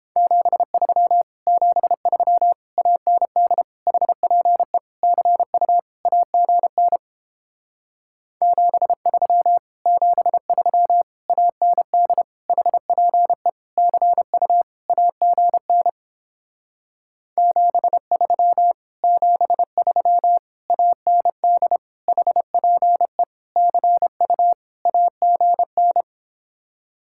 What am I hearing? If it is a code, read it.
Morse code, heard as a keyed tone, about 30 wpm, 700 Hz. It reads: 73 73 AND HPE CU AGN 73 73 AND HPE CU AGN 73 73 AND HPE CU AGN